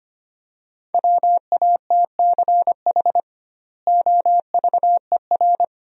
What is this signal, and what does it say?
Morse code, heard as a keyed tone, about 25 wpm, 700 Hz. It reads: WATCH OVER
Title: 2x Word-Spacing - Sentences from Top 300 Words - 25wpm